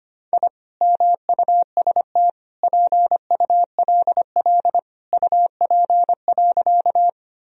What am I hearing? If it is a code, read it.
Morse code, heard as a keyed tone, about 25 wpm, 700 Hz. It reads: I MUST PULL UP.
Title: Sentences from Top 1000 Words - Review Part 2 - 25wpm